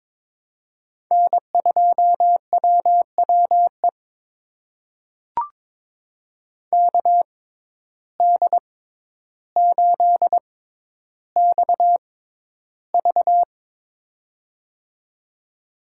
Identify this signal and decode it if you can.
Morse code, heard as a keyed tone, about 21 wpm, spaced as for 11 wpm, 700 Hz. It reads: N2WWE KD8XV
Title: USA Call Signs - 5 Characters - 6x Character Spacing - 22wpm